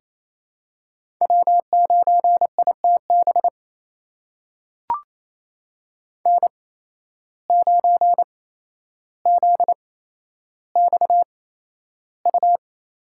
Morse code, heard as a keyed tone, about 28 wpm, spaced as for 12 wpm, 700 Hz. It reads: W9ITB N9ZXU
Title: USA Call Signs - 8x Character Spacing - 28wpm